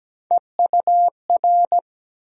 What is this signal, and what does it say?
Morse code, heard as a keyed tone, about 17 wpm, 700 Hz. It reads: EUR
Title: USA Call Sign Postfixes - 3 Letters 17wpm